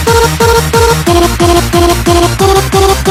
There is no code.